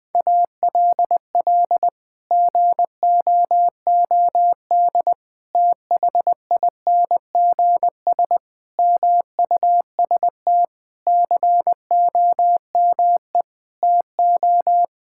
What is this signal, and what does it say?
Morse code, heard as a keyed tone, about 21 wpm, 700 Hz. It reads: ALL GOOD THINGS MUST COME TO
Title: Sets of 8 Words - 20wpm